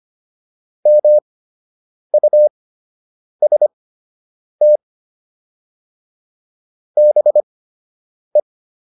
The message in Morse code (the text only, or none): MUST BE